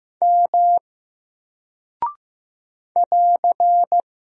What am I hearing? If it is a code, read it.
Morse code, heard as a keyed tone, about 15 wpm, 700 Hz. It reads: M <AR>
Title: Single Character (Letter:Number:Punctuation:Prosign) 15wpm